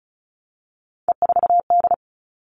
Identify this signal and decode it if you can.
Morse code, heard as a keyed tone, about 34 wpm, 700 Hz. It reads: E4D
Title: All Cut Numbers - 3-Digit 35wpm